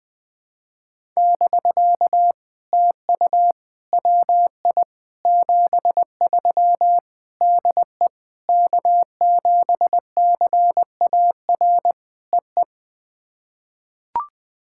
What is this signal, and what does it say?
Morse code, heard as a keyed tone, about 21 wpm, 700 Hz. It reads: <BK> TU WI 73 DE K7CAR EE
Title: POTA QSO Practice - 20wpm